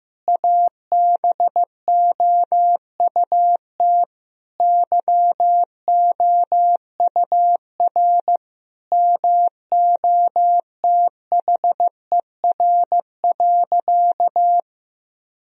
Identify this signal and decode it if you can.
Morse code, heard as a keyed tone, about 15 wpm, 700 Hz. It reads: ABOUT YOUR MOTHER.
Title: Sentences from Top 1000 Words - Review Part 3 - 15wpm